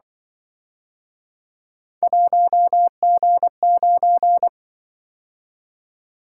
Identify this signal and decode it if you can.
Morse code, heard as a keyed tone, about 24 wpm, 700 Hz. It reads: 1G9
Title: Speed-Racing - Three Letter-Number Combinations - 20wpm